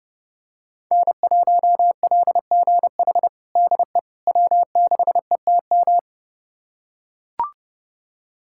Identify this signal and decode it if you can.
Morse code, heard as a keyed tone, about 32 wpm, 700 Hz. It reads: N1LGH DE W6ETM
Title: USA Callsign DE Callsign - 5 Characters - 30wpm